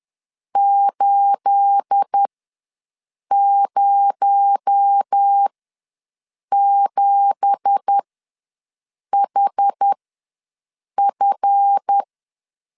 Morse code, heard as a keyed tone, about 10 wpm, spaced as for 6 wpm, 800 Hz. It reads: 807HF
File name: LEZIONI / 9 2867 ws 10wpm 4215